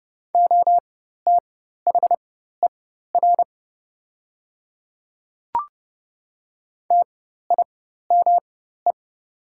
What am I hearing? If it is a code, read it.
Morse code, heard as a keyed tone, about 31 wpm, spaced as for 14 wpm, 700 Hz. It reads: OTHER TIME
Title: 4x Character Spacing - Top 100 Words - 30wpm